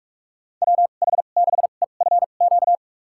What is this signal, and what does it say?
Morse code, heard as a keyed tone, about 43 wpm, spaced as for 32 wpm, 700 Hz. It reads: WH6EFQ
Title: USA Call Signs - 2x Character Spacing - 45wpm